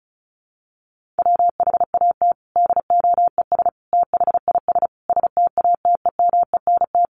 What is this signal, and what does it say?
Morse code, heard as a keyed tone, about 34 wpm, 700 Hz. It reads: WHAT DOES THIS STATEMENT
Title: Sentences from Top 1200 Words 35wpm